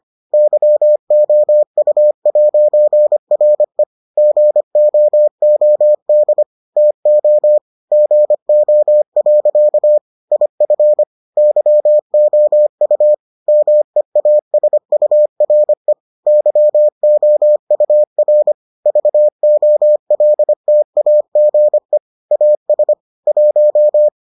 Morse code, heard as a keyed tone, about 25 wpm, 600 Hz. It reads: YOU'RE GOOD TO GO. IF YOU MEASURE YOUR VOLTAGE AS 1